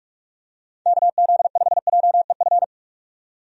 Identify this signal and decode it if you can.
Morse code, heard as a keyed tone, about 44 wpm, 700 Hz. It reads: KZ5JEF
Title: USA Call Signs - 6 Characters - 45wpm